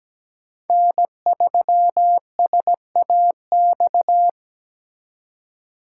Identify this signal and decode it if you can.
Morse code, heard as a keyed tone, about 20 wpm, 700 Hz. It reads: N3SAX
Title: Speed Racing - USA 1x3 Call Signs 17wpm